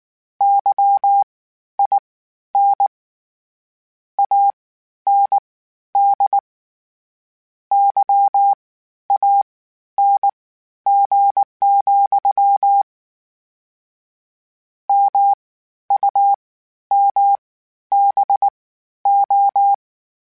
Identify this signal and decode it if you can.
Morse code, heard as a keyed tone, about 19 wpm, spaced as for 11 wpm, 800 Hz. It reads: YIN AND YANG, MUMBO
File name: Misc / Misc-mp3s / WordPairs sp @18